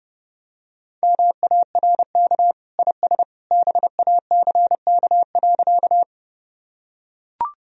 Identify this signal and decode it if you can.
Morse code, heard as a keyed tone, about 30 wpm, 700 Hz. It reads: MARK IS BACK.
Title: Sentences from Top 1000 Words - Review Part 1 - 30wpm